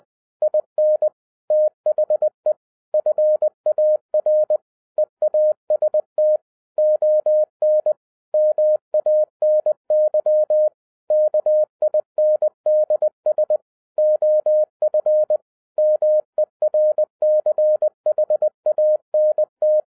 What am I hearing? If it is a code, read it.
Morse code, heard as a keyed tone, about 20 wpm, 610 Hz. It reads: IN THE FAR EAST ON MANY KINDS OF MERCHANT